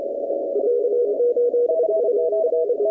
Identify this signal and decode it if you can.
Morse code, heard as a keyed tone, about 27 wpm, 480 Hz, with noise. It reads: K8UR
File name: contests / 2019-cw-wpx-kt5j-k5pi- -2330 0-21